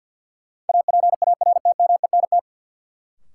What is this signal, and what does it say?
Morse code, heard as a keyed tone, about 47 wpm, 700 Hz. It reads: APARTMENT
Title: Sentences from Top 900 Words - New Word Drill - 50wpm